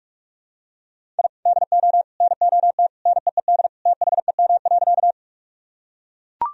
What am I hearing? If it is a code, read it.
Morse code, heard as a keyed tone, about 42 wpm, 700 Hz. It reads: I DO NOT NEED THEM.
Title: Sentences from Top 1000 Words - Review Part 1 - 45wpm